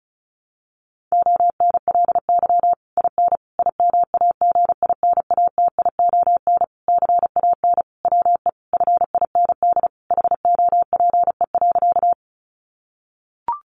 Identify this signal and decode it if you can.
Morse code, heard as a keyed tone, about 34 wpm, 700 Hz. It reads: ONLY IN IMAGINATION CAN WE FIND HOPE.
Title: Sentences from Top 1300 Words 35wpm